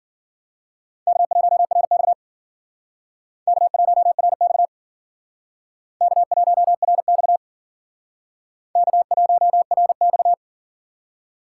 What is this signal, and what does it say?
Morse code, heard as a keyed tone, about 54 wpm, 700 Hz. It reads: K1RX K1RX K1RX K1RX